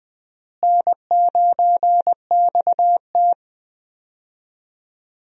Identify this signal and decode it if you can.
Morse code, heard as a keyed tone, about 21 wpm, 700 Hz. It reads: N9XT
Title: USA Call Signs - 4 Characters - Repeated 3x - 20wpm